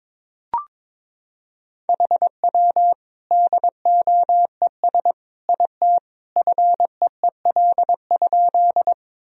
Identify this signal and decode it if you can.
Morse code, heard as a keyed tone, about 22 wpm, 700 Hz. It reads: HW DOES IT FEEL?
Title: Sentences from Top 400 Words and Abbreviations 22wpm